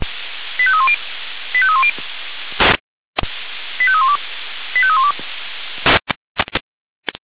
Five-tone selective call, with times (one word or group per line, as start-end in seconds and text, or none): 0.6-1.0 85310
1.5-1.9 85310
3.8-4.2 85312
4.8-5.1 85312